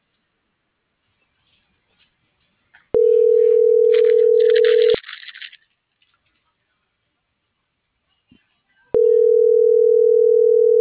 Ringback tone (North America), 440 + 480 Hz.
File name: CENTRAL SAGOT / out diego.emir